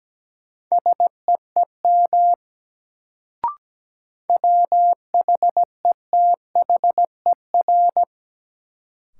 Morse code, heard as a keyed tone, about 17 wpm, 700 Hz. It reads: SEEM WHETHER
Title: Sets of 3 Words - New Word Drill - 17wpm